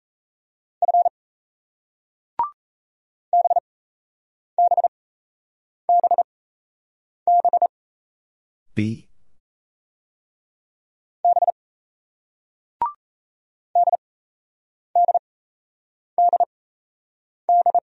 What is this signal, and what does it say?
Morse code, heard as a keyed tone, about 37 wpm, spaced as for 8 wpm, 700 Hz. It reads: F BBBB B DDDD